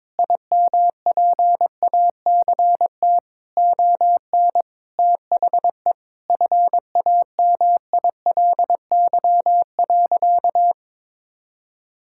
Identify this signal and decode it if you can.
Morse code, heard as a keyed tone, about 21 wpm, 700 Hz. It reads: IMPACT ON THE FAMILY.
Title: Sentences from Top 1000 Words - Review Part 6 - 22wpm